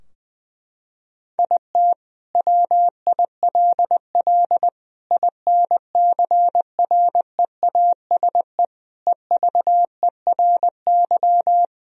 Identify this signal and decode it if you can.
Morse code, heard as a keyed tone, about 19 wpm, 700 Hz. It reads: IT WILL INCREASE EVERY
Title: Sentences from Top 1000 Words - Review Part 4 - 20wpm